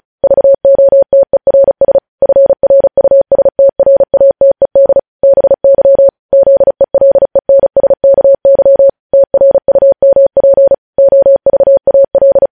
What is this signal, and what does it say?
Morse code, heard as a keyed tone, about 36 wpm, 550 Hz. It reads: VOTERS FRUSTRATED BY ZELENSKY TRUMP OVAL
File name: podcast / News headlines-1x-35x35